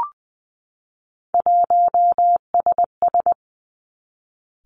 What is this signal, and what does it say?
Morse code, heard as a keyed tone, about 20 wpm, 700 Hz. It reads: 1SS
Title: Three Challenging Letter-Number Combinations 20wpm